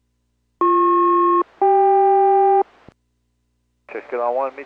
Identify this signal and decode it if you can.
Aircraft SELCAL, heard as two pairs of tones, with speech. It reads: BP-CK